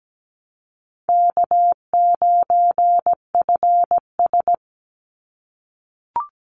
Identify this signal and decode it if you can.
Morse code, heard as a keyed tone, about 17 wpm, 700 Hz. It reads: K9FS
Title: Easy Call Signs - 17wpm